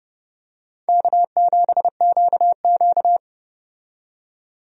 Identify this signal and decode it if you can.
Morse code, heard as a keyed tone, about 32 wpm, 700 Hz. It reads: K7QQ
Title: USA Call Signs - 4 Characters - 30wpm